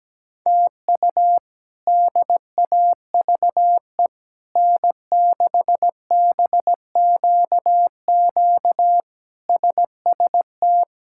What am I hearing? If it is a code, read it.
Morse code, heard as a keyed tone, about 17 wpm, 700 Hz. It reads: TU DAVE N6BQQ SST
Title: K1USN SST QSO Practice - 17wpm